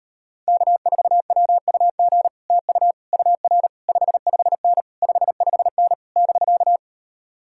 Morse code, heard as a keyed tone, about 39 wpm, 700 Hz. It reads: K4WUG TU UR 55N 55N <BK>